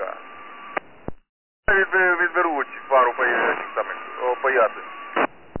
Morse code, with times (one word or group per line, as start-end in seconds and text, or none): none